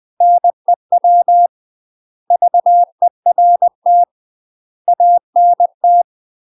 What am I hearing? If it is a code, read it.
Morse code, heard as a keyed tone, about 20 wpm, 700 Hz. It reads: NEW VERT ANT
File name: UK MASTER QSOs TRAINING/20WPM